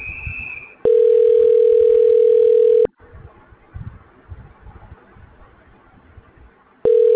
Ringback tone (North America), 440 + 480 Hz.